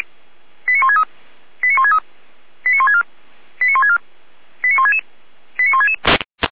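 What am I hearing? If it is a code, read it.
Five-tone selective call (ZVEI), heard as quick runs of five tones, tones of about 80 ms. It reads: 89163 89163 89164 89164 89166 89166
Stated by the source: ZVEI five-tone selective call received on an SDR.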